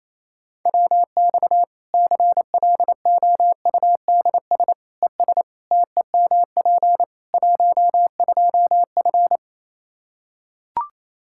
Morse code, heard as a keyed tone, about 28 wpm, 700 Hz. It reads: WX CLOUDS ES TEMP 12F
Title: K1USN SST QSO Practice 28wpm